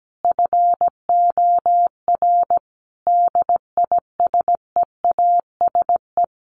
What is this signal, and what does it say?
Morse code, heard as a keyed tone, about 17 wpm, 700 Hz. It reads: FOR DISEASE
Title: Sets of 5 Words Encore2 17wpm